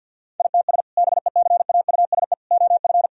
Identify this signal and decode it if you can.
Morse code, heard as a keyed tone, about 50 wpm, 700 Hz. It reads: ITS BECAUSE OF
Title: Sentences from Top 1000 Words - Review Part 3 - 50wpm